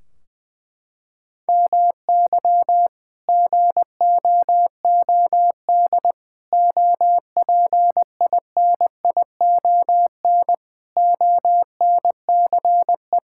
Morse code, heard as a keyed tone, about 20 wpm, 700 Hz. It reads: MY GOOD OPINION ONCE
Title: Sentences from Top 1600 Words 20wpm